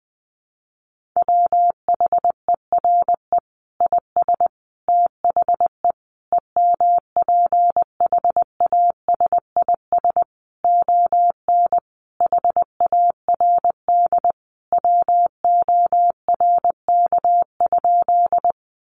Morse code, heard as a keyed tone, about 19 wpm, 700 Hz. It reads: WHERE IS THE EMPHASIS ON HARD WORK?
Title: Sentences from Top 1400 Words 20wpm